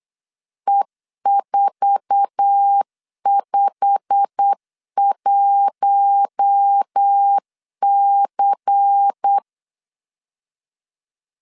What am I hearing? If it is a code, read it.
Morse code, heard as a keyed tone, about 8 wpm, 800 Hz. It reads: E451C
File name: LEZIONI / 8 YJ45 / 8 YJ45 8wpm 3214